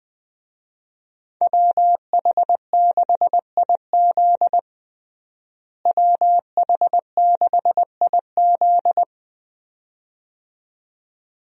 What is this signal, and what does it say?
Morse code, heard as a keyed tone, about 21 wpm, 700 Hz. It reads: WH6IZ WH6IZ
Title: USA Call Signs - 5 Characters - Repeated 2x - 20wpm